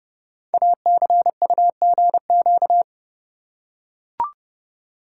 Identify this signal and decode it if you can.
Morse code, heard as a keyed tone, about 30 wpm, 700 Hz. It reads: ACUGQ
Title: Code Groups - 5 Letters - 30wpm